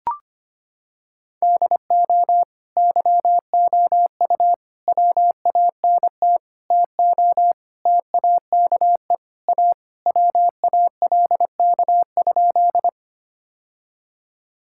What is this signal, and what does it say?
Morse code, heard as a keyed tone, about 25 wpm, 700 Hz. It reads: DO YOU WANT TO TAKE A WALK?
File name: Sentences with 3-Word Phrases - 25wpm